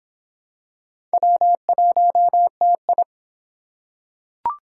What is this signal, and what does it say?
Morse code, heard as a keyed tone, about 22 wpm, 700 Hz. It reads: W1TI